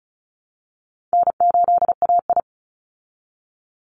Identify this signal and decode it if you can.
Morse code, heard as a keyed tone, about 34 wpm, 700 Hz. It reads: N8AI
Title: Call Signs Repeated 3x - 35wpm